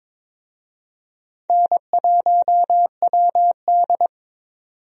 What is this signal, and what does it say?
Morse code, heard as a keyed tone, about 21 wpm, 700 Hz. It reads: N1WD
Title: USA Call Signs - 4 Characters - 22wpm